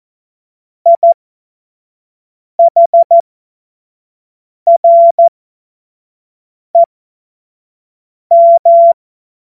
Morse code, heard as a keyed tone, about 14 wpm, spaced as for 5 wpm, 680 Hz. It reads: IHREM
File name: Morsekurs / Witze / Witz 17. August 2021, 25 F70